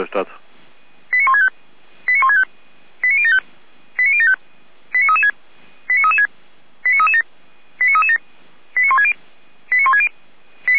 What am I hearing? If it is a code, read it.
Five-tone selective call (ZVEI), heard as quick runs of five tones, tones of about 70 ms, with speech. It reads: 89165 89165 89075 89075 89336 89336 89337 89337 89160 89160